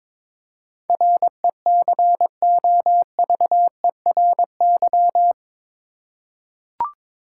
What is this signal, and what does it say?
Morse code, heard as a keyed tone, about 22 wpm, 700 Hz. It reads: RECOVERY
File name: practice / 22wpm / Eight Letter Words - 22wpm